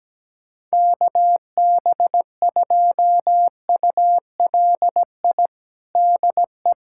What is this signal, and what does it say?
Morse code, heard as a keyed tone, about 17 wpm, 700 Hz. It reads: KB2ULI DE
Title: USA Callsign DE Callsign 17wpm